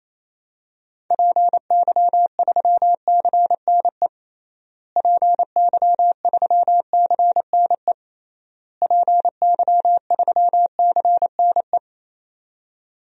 Morse code, heard as a keyed tone, about 28 wpm, 700 Hz. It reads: PY3CNE PY3CNE PY3CNE
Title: DXCC-100 Call Signs - 6 Characters - Repeated 3x - 28wpm